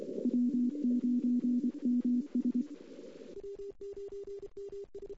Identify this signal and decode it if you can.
Morse code, heard as a keyed tone, about 24 wpm, 250 Hz. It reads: W9MS
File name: CW / 2008-ss-cw-k5tr-op-n5rz -03:00